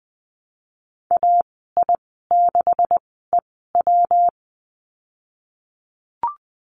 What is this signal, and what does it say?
Morse code, heard as a keyed tone, about 20 wpm, spaced as for 15 wpm, 700 Hz. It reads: AI6EW
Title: 179 Lesson 29g - Slash - Call Signs 20wpm